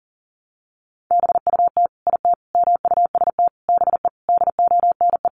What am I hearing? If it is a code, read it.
Morse code, heard as a keyed tone, about 40 wpm, 700 Hz. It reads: BUT IT MUST BE DONE
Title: Sentences from Top 500 Words 40wpm